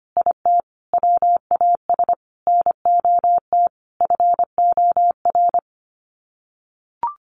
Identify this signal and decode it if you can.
Morse code, heard as a keyed tone, about 25 wpm, 700 Hz. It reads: IT WAS NOT FOR
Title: Sets of 5 Words Encore 25wpm